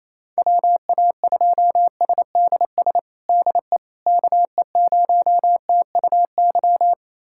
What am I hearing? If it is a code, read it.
Morse code, heard as a keyed tone, about 28 wpm, 700 Hz. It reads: WA2SDS DE KE0TUY
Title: USA Callsign DE Callsign - 6 Characters - 28wpm